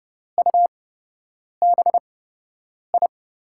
Morse code, heard as a keyed tone, about 32 wpm, spaced as for 8 wpm, 700 Hz. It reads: UBI
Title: USA Call Signs - 6 Characters - 8x Character Spacing - 30wpm